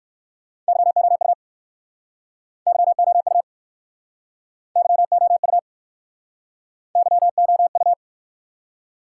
Morse code, heard as a keyed tone, about 60 wpm, 700 Hz. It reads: YOU YOU YOU YOU